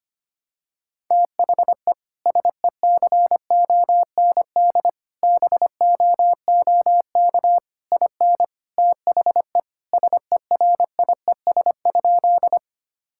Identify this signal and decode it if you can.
Morse code, heard as a keyed tone, about 25 wpm, 700 Hz. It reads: THE SECOND BOOK IN THE SERIES?
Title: Sentences from Top 1000 Words - Review Part 5 - 25wpm